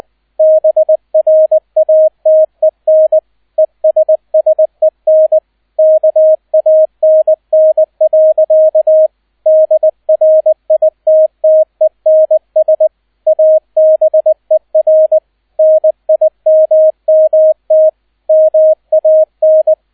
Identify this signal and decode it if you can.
Morse code, heard as a keyed tone, about 20 wpm, 630 Hz, with radio noise. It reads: BRATEN ESSEN KANN. DRITTENS ABER NIMMT MAN